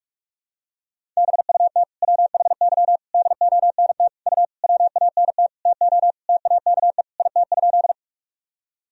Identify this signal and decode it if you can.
Morse code, heard as a keyed tone, about 47 wpm, 700 Hz. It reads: BUT WHY DONT U WANT TO TAKE IT?
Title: Sentences from Top 300 Words and Abbreviations 45wpm